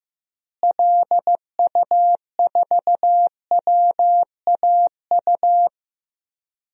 Morse code, heard as a keyed tone, about 16 wpm, 700 Hz. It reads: LU4WAU
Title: DXCC-100 Call Signs - 6 Characters - Repeated 3x - 15wpm